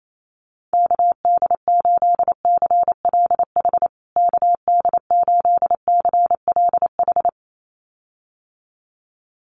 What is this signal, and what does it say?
Morse code, heard as a keyed tone, about 28 wpm, 700 Hz. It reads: KD8CLH KD8CLH
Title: USA Call Signs Repeated 2x 28wpm